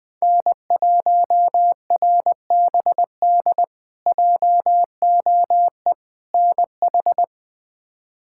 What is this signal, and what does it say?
Morse code, heard as a keyed tone, about 21 wpm, 700 Hz. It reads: N1RBD JOE NH
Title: K1USN SST QSO Practice - 20wpm